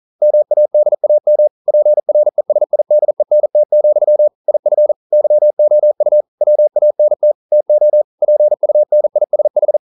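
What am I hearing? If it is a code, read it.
Morse code, heard as a keyed tone, about 42 wpm, 600 Hz. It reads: MADAM PRESIDENT, IF YOU WANT TO PUNISH
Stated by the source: Morse code practice at 41 wpm, from CW med Gnister.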